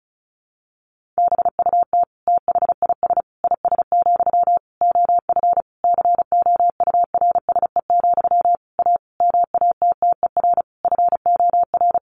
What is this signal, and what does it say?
Morse code, heard as a keyed tone, about 34 wpm, 700 Hz. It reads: BUT THIS IS, OF COURSE, A MATTER FOR
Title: Sentences from Top 1500 Words 35wpm